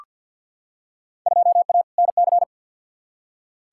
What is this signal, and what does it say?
Morse code, heard as a keyed tone, about 51 wpm, 700 Hz. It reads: 2A NC
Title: Field Day Exchanges Encore - 50wpm